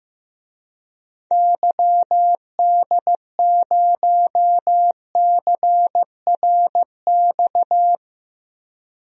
Morse code, heard as a keyed tone, about 15 wpm, 700 Hz. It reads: YD0CRX
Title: DXCC-100 Call Signs - 6 Characters - 15wpm